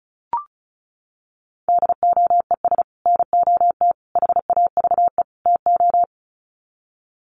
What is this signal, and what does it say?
Morse code, heard as a keyed tone, about 34 wpm, 700 Hz. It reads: DOES NOT HAVE TO
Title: Sets of 4 Words 35wpm